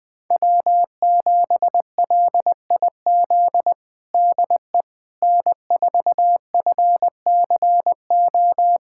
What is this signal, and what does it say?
Morse code, heard as a keyed tone, about 20 wpm, 700 Hz. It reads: W7LIZ DE N4FCO
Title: USA Callsign DE Callsign - 5 Characters - 20wpm